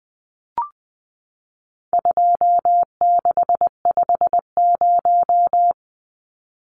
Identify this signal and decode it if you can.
Morse code, heard as a keyed tone, about 20 wpm, 700 Hz. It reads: 2650
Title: Numbers - 4-Digit 20wpm